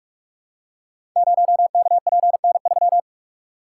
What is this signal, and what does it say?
Morse code, heard as a keyed tone, about 44 wpm, 700 Hz. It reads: 0KPN3
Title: Code Groups - 5 Letter:Numbers - 45wpm